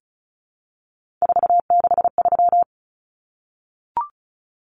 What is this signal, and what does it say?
Morse code, heard as a keyed tone, about 34 wpm, 700 Hz. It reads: 463
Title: Common Cut Numbers - 3-Digit 35wpm